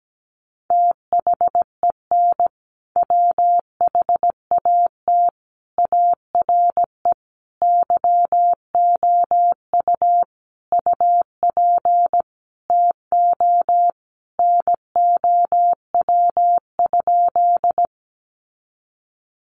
Morse code, heard as a keyed tone, about 17 wpm, 700 Hz. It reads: THEN WHAT ARE YOU UP TO NOW?
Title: Sentences from Top 100 Words 17wpm